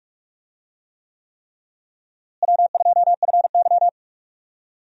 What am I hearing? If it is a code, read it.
Morse code, heard as a keyed tone, about 46 wpm, 700 Hz. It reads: W2FY
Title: USA Call Signs - 4 Characters - Repeated 3x - 45wpm